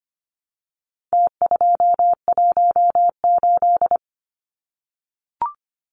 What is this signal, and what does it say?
Morse code, heard as a keyed tone, about 25 wpm, 700 Hz. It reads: T218
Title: Numbers - Contest Style - 4-Digit 25wpm